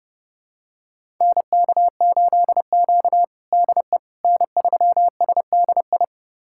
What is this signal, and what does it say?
Morse code, heard as a keyed tone, about 30 wpm, 700 Hz. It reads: NK8Q DE N3SDI
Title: USA Callsign DE Callsign 30wpm